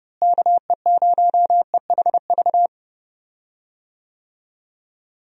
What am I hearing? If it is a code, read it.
Morse code, heard as a keyed tone, about 30 wpm, 700 Hz. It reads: KE0EHV
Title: USA Call Signs - 6 Characters - Repeated 2x - 30wpm